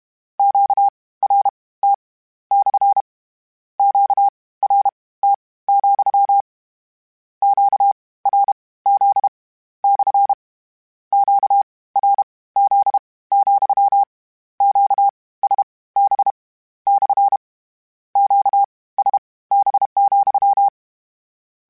Morse code, heard as a keyed tone, about 32 wpm, spaced as for 19 wpm, 800 Hz. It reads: QRT / QRT, QRZ / QRZ, QSB / QSB,
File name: Q Calls / Q Calls-mp3s / Q Calls SPSL2 @30